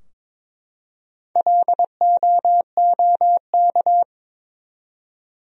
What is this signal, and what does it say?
Morse code, heard as a keyed tone, about 22 wpm, 700 Hz. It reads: LOOK
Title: 4x Word-Spacing - Sentences from Top 300 Words - 22wpm